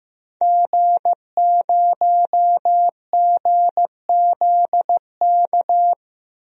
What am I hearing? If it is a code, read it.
Morse code, heard as a keyed tone, about 15 wpm, 700 Hz. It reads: G0GZK
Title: DXCC-100 Call Signs - 5 Characters - Repeated 3x - 15wpm